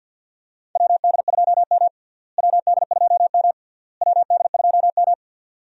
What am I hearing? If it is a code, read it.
Morse code, heard as a keyed tone, about 52 wpm, 700 Hz. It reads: WD2M WD2M WD2M
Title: USA Call Signs - 4 Characters - Repeated 3x - 50wpm